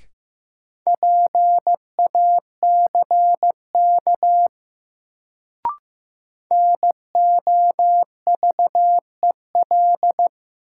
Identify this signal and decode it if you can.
Morse code, heard as a keyed tone, about 16 wpm, 700 Hz. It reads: PACK NOVEL
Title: Sentences from Top 1300 Words - New Word Drill - 15wpm